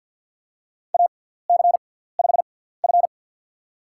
Morse code, heard as a keyed tone, about 50 wpm, spaced as for 17 wpm, 700 Hz. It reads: AC5F
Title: USA Call Signs - 4 Characters - 6x Character Spacing - 50wpm